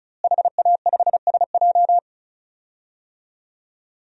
Morse code, heard as a keyed tone, about 34 wpm, 700 Hz. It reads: HA5SJ
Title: DXCC-100 Call Signs - 5 Characters - Repeated 3x - 35wpm